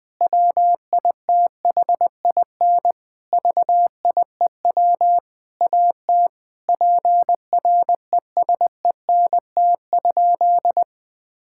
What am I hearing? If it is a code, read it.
Morse code, heard as a keyed tone, about 20 wpm, 700 Hz. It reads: WITHIN VIEW AT PRESENT?
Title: Sentences from Top 1000 Words - Review Part 6 - 20wpm